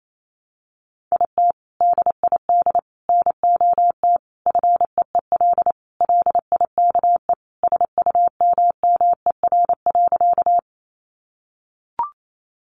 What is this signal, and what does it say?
Morse code, heard as a keyed tone, about 28 wpm, 700 Hz. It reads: IT DID NOT FEEL LIKE SUMMER.
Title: Sentences from Top 1100 Words 28wpm